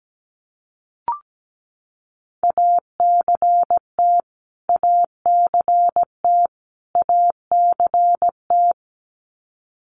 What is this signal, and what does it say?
Morse code, heard as a keyed tone, about 17 wpm, 700 Hz. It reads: ACT ACT ACT
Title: Top 200 Words Repeated 17wpm